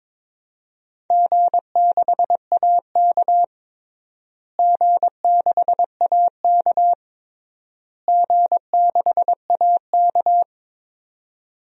Morse code, heard as a keyed tone, about 21 wpm, 700 Hz. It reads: G6AK G6AK G6AK
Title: DXCC-100 Call Signs - 4 Characters - Repeated 3x - 22wpm